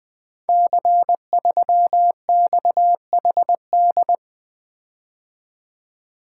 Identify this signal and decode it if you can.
Morse code, heard as a keyed tone, about 20 wpm, 700 Hz. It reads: C3XHD
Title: Code Groups - 5 Letter:Numbers - 20wpm